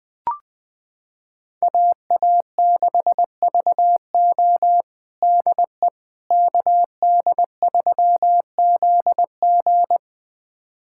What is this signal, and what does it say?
Morse code, heard as a keyed tone, about 20 wpm, 700 Hz. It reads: AA6VO DE KD3ZG
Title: USA Callsign DE Callsign - 5 Characters - 20wpm